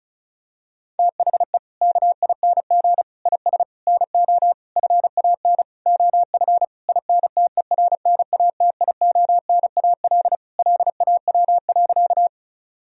Morse code, heard as a keyed tone, about 36 wpm, 700 Hz. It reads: THE KING IS NO FAN OF INTERNATIONAL LAW.
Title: Sentences from Top 1000 Words - Review Part 4 - 35wpm